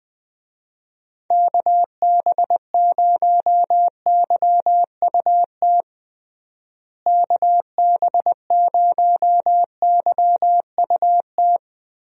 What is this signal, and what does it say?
Morse code, heard as a keyed tone, about 20 wpm, 700 Hz. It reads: KB0YUT KB0YUT
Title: USA Call Signs - 6 Characters - Repeated 2x - 20wpm